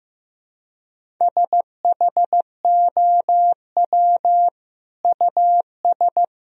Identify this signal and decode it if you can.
Morse code, heard as a keyed tone, about 15 wpm, 700 Hz. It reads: SHOW US
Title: Sentences from Top 200 Words and Abbreviations 15wpm